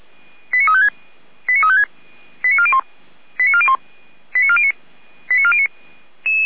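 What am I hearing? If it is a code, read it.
Five-tone selective call (ZVEI), heard as quick runs of five tones, tones of about 70 ms. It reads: 89356 89356 89401 89401 89409 89409
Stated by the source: ZVEI five-tone call received by SDR, Feuerwehr Orlamünde.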